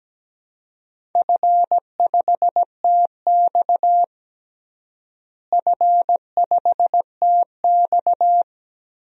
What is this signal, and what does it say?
Morse code, heard as a keyed tone, about 17 wpm, 700 Hz. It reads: F5TX F5TX
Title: DXCC-100 Call Signs - 4 Characters - Repeated 2x - 17wpm